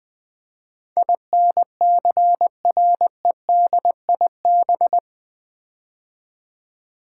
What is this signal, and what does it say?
Morse code, heard as a keyed tone, about 20 wpm, 700 Hz. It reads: INCREDIB
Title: Build A Word - 4 Letters At A Time - 10-Letter Words - 20wpm